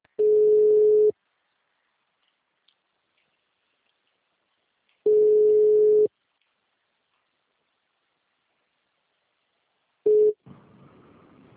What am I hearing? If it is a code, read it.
Ringback tone (Europe, 425 Hz).
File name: out- -2013- - - .83